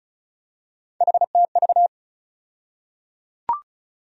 Morse code, heard as a keyed tone, about 34 wpm, 700 Hz. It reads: HTV
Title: USA Call Sign Postfixes - 3 Letters 35wpm